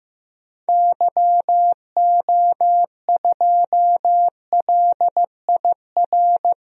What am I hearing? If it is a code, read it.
Morse code, heard as a keyed tone, about 15 wpm, 700 Hz. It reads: YO2LIR
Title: DXCC-100 Call Signs - 6 Characters - 15wpm